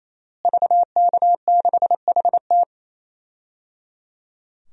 Morse code, heard as a keyed tone, about 28 wpm, 700 Hz. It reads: VK6HT
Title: DXCC-100 Call Signs - 5 Characters - Repeated 3x - 28wpm